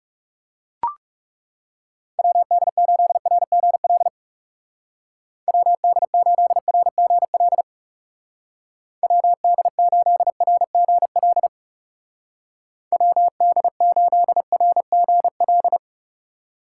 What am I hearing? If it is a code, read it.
Morse code, heard as a keyed tone, about 40 wpm, 700 Hz. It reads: WD8RGL WD8RGL WD8RGL WD8RGL